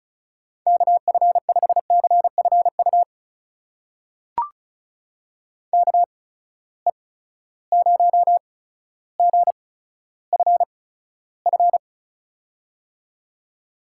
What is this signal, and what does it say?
Morse code, heard as a keyed tone, about 36 wpm, spaced as for 15 wpm, 700 Hz. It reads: KF5CFU KE0GFF
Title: USA Call Signs - 6 Characters - 8x Character Spacing - 35wpm